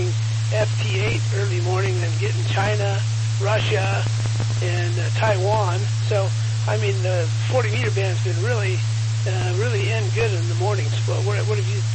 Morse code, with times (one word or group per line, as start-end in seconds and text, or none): none